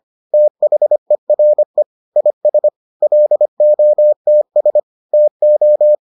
Morse code, heard as a keyed tone, about 25 wpm, 600 Hz. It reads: THERE IS LOTS TO